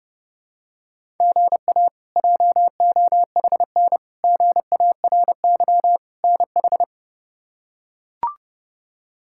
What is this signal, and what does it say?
Morse code, heard as a keyed tone, about 32 wpm, 700 Hz. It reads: GA JOHN GARY NH
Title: K1USN SST QSO Practice - 30wpm